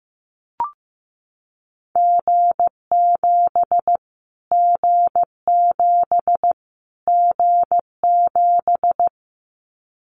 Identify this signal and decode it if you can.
Morse code, heard as a keyed tone, about 15 wpm, 700 Hz. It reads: G7 G7 G7
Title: Callsign Prefixes for Top 25 DXCC - Repeated 3x - 15wpm